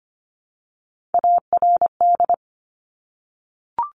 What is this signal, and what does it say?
Morse code, heard as a keyed tone, about 25 wpm, 700 Hz. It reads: ARD
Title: Top Three Letter Combinations in Words 25wpm